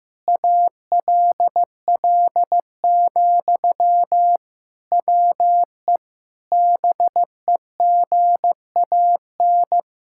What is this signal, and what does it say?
Morse code, heard as a keyed tone, about 16 wpm, 700 Hz. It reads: ALL, WE BEGAN
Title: Sentences from Top 1600 Words 15wpm